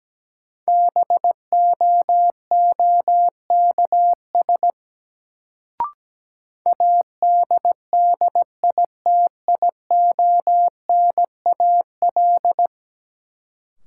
Morse code, heard as a keyed tone, about 17 wpm, 700 Hz. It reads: BOOKS ADDITIONAL
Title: Sentences from Top 1000 Words - Review Part 4 - New Word Drill - 17wpm